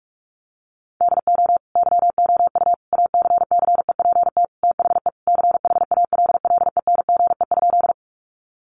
Morse code, heard as a keyed tone, about 44 wpm, 700 Hz. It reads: DO YOU ACCEPT THE CHALLENGE?
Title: Sentences from Top 1300 Words 45wpm